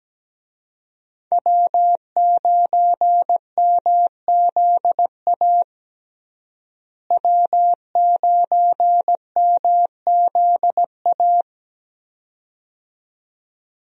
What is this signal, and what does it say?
Morse code, heard as a keyed tone, about 17 wpm, 700 Hz. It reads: W9MZA W9MZA
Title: USA Call Signs - 5 Characters - Repeated 2x - 17wpm